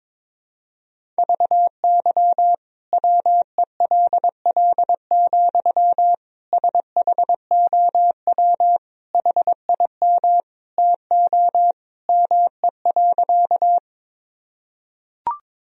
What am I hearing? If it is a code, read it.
Morse code, heard as a keyed tone, about 22 wpm, 700 Hz. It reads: VY WELL, SHOW HIM TO ME.
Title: Sentences from Top 200 Words and Abbreviations 22wpm